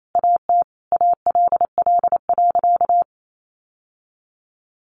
Morse code, heard as a keyed tone, about 28 wpm, 700 Hz. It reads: AT ALL.
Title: Sentences from Top 1100 Words 28wpm